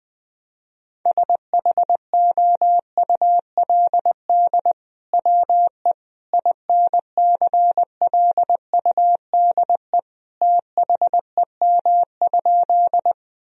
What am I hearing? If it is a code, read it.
Morse code, heard as a keyed tone, about 20 wpm, 700 Hz. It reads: SHOULD WE INCLUDE THEM?
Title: Sentences from Top 1000 Words - Review Part 3 - 20wpm